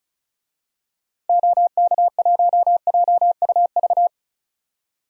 Morse code, heard as a keyed tone, about 34 wpm, 700 Hz. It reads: OK1JUV
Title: DXCC-100 Call Signs - 6 Characters - 35wpm